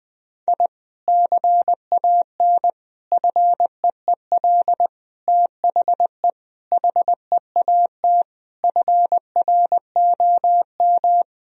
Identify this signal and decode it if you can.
Morse code, heard as a keyed tone, about 21 wpm, 700 Hz. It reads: I CAN FEEL THE HEAT FROM
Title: Sentences from Top 1000 Words - Review Part 3 - 20wpm